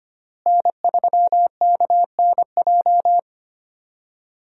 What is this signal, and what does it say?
Morse code, heard as a keyed tone, about 25 wpm, 700 Hz. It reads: N3KNJ
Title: USA Call Signs - 5 Characters - Repeated 2x - 25wpm